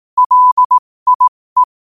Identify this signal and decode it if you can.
Morse code, heard as a keyed tone, about 18 wpm, spaced as for 15 wpm, 1000 Hz. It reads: LIE
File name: qotdfiles / 15wpm